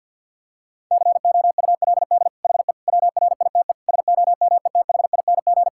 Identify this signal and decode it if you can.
Morse code, heard as a keyed tone, about 50 wpm, 700 Hz. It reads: COULD HE WRITE SOMETHING
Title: Sentences from Top 1000 Words - Review Part 6 - 50wpm